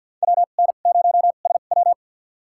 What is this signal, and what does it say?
Morse code, heard as a keyed tone, about 47 wpm, spaced as for 35 wpm, 700 Hz. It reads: WN0SW
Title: USA Call Signs - 5 Characters - 2x Character Spacing - 50wpm